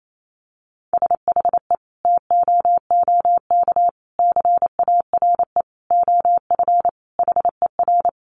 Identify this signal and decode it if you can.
Morse code, heard as a keyed tone, about 28 wpm, 700 Hz. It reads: SHE TOOK CARE OF HER
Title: Sentences from Top 1300 Words 28wpm